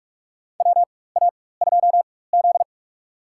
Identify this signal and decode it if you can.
Morse code, heard as a keyed tone, about 43 wpm, spaced as for 21 wpm, 700 Hz. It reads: WA2Z
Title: USA Call Signs - 4x Character Spacing - 45wpm